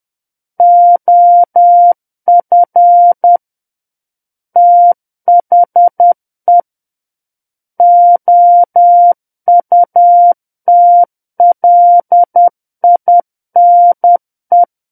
Morse code, heard as a keyed tone, about 10 wpm, 700 Hz. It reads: OF THE OUTLINE
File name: Book One / 01-Chapter 1 - 10 WPM